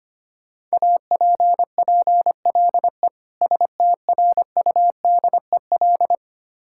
Morse code, heard as a keyed tone, about 25 wpm, 700 Hz. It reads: APPLE STRUDEL
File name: practice / 25wpm / Difficult Words with Hints - 25wpm